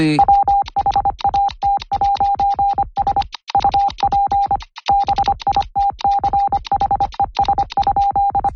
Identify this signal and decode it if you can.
Morse code, heard as a keyed tone, about 25 wpm, 800 Hz. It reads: WHAT'S UP BITCHES?